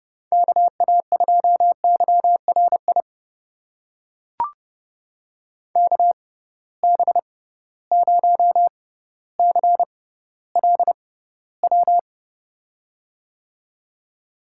Morse code, heard as a keyed tone, about 31 wpm, spaced as for 16 wpm, 700 Hz. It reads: KA2YRI KB0CLW
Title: USA Call Signs - 6 Characters - 6x Character Spacing - 30wpm